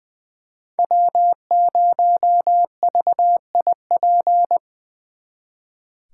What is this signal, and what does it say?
Morse code, heard as a keyed tone, about 23 wpm, 700 Hz. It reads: W0VIP
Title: Speed Racing - USA 1x3 Call Signs 20wpm